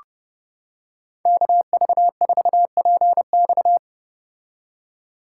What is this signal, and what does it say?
Morse code, heard as a keyed tone, about 30 wpm, 700 Hz. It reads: KV4PX